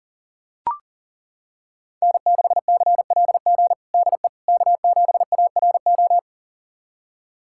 Morse code, heard as a keyed tone, about 40 wpm, 700 Hz. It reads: N6CLG DE K7ARO
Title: USA Callsign DE Callsign - 5 Characters - 40wpm